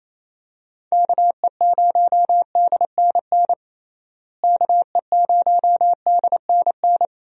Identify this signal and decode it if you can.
Morse code, heard as a keyed tone, about 27 wpm, 700 Hz. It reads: KE0DNN KE0DNN
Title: USA Call Signs - 6 Characters - Repeated 2x - 28wpm